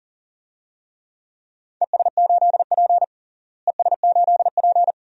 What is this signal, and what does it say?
Morse code, heard as a keyed tone, about 40 wpm, 700 Hz. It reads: ES8P ES8P
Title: DXCC-100 Call Signs - 4 Characters - Repeated 2x - 40wpm